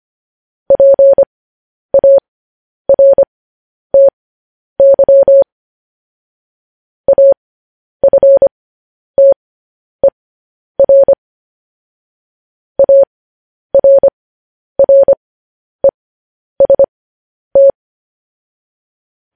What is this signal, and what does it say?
Morse code, heard as a keyed tone, about 25 wpm, spaced as for 10 wpm, 550 Hz. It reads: PARTY AFTER ARREST
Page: news headlines in morse code at 10 wpm